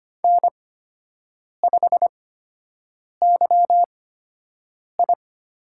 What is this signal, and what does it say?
Morse code, heard as a keyed tone, about 25 wpm, spaced as for 7 wpm, 700 Hz. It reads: N5YI
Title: USA Call Signs - 4 Characters - 8x Character Spacing - 25wpm